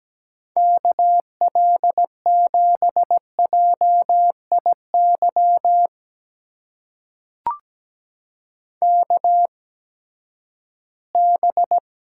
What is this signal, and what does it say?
Morse code, heard as a keyed tone, about 17 wpm, spaced as for 12 wpm, 700 Hz. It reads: KL7JIY KB
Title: USA Call Signs - 6 Characters - 8x Character Spacing - 17wpm